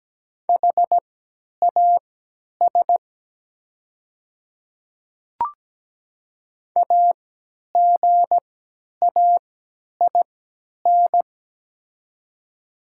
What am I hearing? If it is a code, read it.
Morse code, heard as a keyed tone, about 17 wpm, spaced as for 10 wpm, 700 Hz. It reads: HAS AGAIN
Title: 3x Character Spacing - Top 200 Words - 17wpm